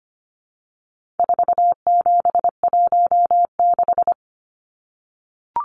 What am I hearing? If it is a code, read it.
Morse code, heard as a keyed tone, about 25 wpm, 700 Hz. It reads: 4716
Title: Numbers - 4-Digit 25wpm